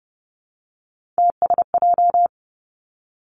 Morse code, heard as a keyed tone, about 30 wpm, 700 Hz. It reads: TSJ
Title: Three Challenging Letter-Number Combinations 30wpm